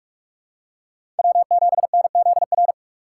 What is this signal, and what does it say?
Morse code, heard as a keyed tone, about 46 wpm, 700 Hz. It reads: W7NZR
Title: USA Call Signs - 5 Characters - 3x Character Spacing - 45wpm